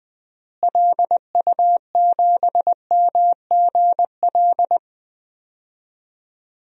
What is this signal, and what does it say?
Morse code, heard as a keyed tone, about 20 wpm, 700 Hz. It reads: LU7MGL
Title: DXCC-100 Call Signs - 6 Characters - Repeated 2x - 20wpm